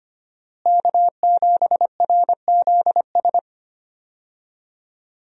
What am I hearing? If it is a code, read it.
Morse code, heard as a keyed tone, about 25 wpm, 700 Hz. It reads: K7RZS
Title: USA Call Signs - 5 Characters - Repeated 3x - 25wpm